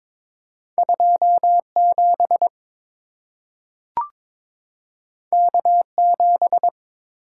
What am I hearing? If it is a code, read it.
Morse code, heard as a keyed tone, about 22 wpm, 700 Hz. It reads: 27 K7
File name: practice / 22wpm / Two Letter-Number Combinations - 22wpm